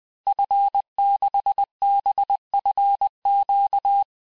Morse code, heard as a keyed tone, about 20 wpm, 800 Hz. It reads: F6BFQ
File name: cours-du-mercredi / audios / cours-858-Fin-de-QSO-vitesse-19mpm